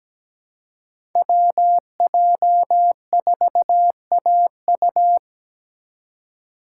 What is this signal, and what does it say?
Morse code, heard as a keyed tone, about 17 wpm, 700 Hz. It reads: WJ4AU
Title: Code Groups - 5 Letter:Numbers - 17wpm